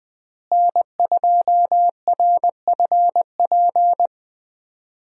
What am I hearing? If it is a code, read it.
Morse code, heard as a keyed tone, about 21 wpm, 700 Hz. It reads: N2RFP
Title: USA Call Signs - 5 Characters - Repeated 3x - 20wpm